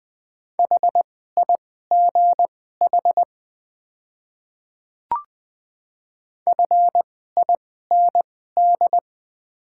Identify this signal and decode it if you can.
Morse code, heard as a keyed tone, about 20 wpm, spaced as for 14 wpm, 700 Hz. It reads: HIGH FIND
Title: 2x Character Spacing - Top 200 Words - 20wpm